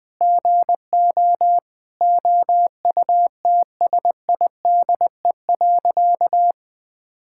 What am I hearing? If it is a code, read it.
Morse code, heard as a keyed tone, about 20 wpm, 700 Hz. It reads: GO OUTSIDE.
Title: Sentences with 5-Word Phrases - Spoken Beginning - 20wpm